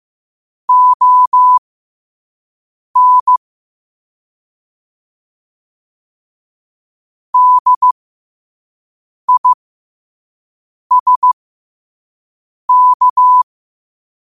Morse code, heard as a keyed tone, about 15 wpm, spaced as for 5 wpm, 1000 Hz. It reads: ON DISK
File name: qotdfiles / 5wpm